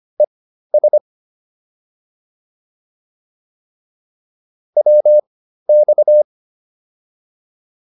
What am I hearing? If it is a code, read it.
Morse code, heard as a keyed tone, about 25 wpm, spaced as for 13 wpm, 600 Hz. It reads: ES WX